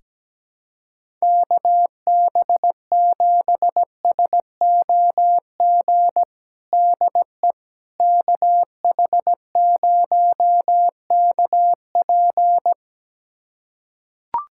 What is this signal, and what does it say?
Morse code, heard as a keyed tone, about 17 wpm, 700 Hz. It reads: KB7SOG DE KH0KP
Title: USA Callsign DE Callsign 17wpm